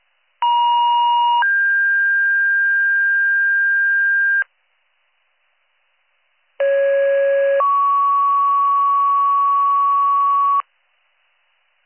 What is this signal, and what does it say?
Two-tone page, heard as a one-second tone, then a three-second tone, about 950 then 1600 Hz, about 550 then 1120 Hz. It is heard from 0.4 to 10.6 s.